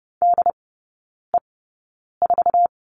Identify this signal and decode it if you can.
Morse code, heard as a keyed tone, about 30 wpm, spaced as for 9 wpm, 700 Hz. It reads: DE4
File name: Lesson 18e - Letter m - Three Characters 30wpm